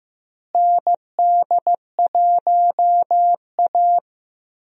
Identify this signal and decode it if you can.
Morse code, heard as a keyed tone, about 15 wpm, 700 Hz. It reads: ND1A
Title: USA Call Signs - 4 Characters - Repeated 3x - 15wpm